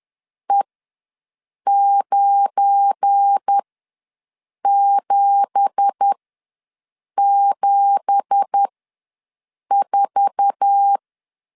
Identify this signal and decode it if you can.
Morse code, heard as a keyed tone, about 10 wpm, spaced as for 6 wpm, 800 Hz. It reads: E9774
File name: LEZIONI / 9 2867 ws 10wpm 4215